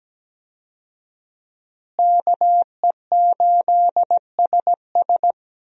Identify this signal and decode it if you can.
Morse code, heard as a keyed tone, about 17 wpm, 700 Hz. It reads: KE8SS
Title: USA Call Signs - 5 Characters - Repeated 2x - 17wpm